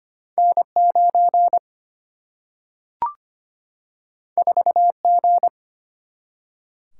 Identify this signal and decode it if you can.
Morse code, heard as a keyed tone, about 25 wpm, 700 Hz. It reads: N9 4G
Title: Single-Letter Single-Number 25wpm